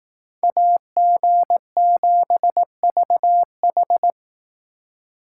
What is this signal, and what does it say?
Morse code, heard as a keyed tone, about 18 wpm, 700 Hz. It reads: AG7VH
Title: Speed Racing - USA 2x2 Call Signs 15wpm